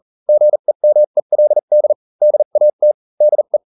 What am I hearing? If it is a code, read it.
Morse code, heard as a keyed tone, about 40 wpm, spaced as for 35 wpm, 600 Hz. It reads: GEMELD DAT DE